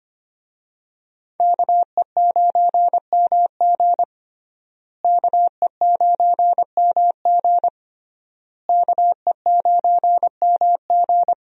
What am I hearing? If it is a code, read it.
Morse code, heard as a keyed tone, about 25 wpm, 700 Hz. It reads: KE9MG KE9MG KE9MG
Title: USA Call Signs - 5 Characters - Repeated 3x - 25wpm